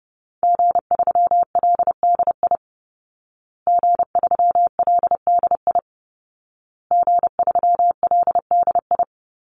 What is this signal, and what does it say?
Morse code, heard as a keyed tone, about 30 wpm, 700 Hz. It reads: G3LDI G3LDI G3LDI
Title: Call Signs Repeated 2x - 30wpm